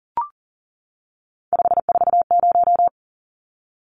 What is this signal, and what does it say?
Morse code, heard as a keyed tone, about 40 wpm, 700 Hz. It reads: 540
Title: Numbers - 3-Digit 40wpm